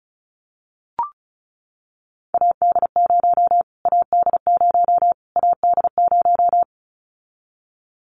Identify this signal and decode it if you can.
Morse code, heard as a keyed tone, about 34 wpm, 700 Hz. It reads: AD0 AD0 AD0
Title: Callsign Prefixes for Top 25 DXCC - Repeated 3x - 35wpm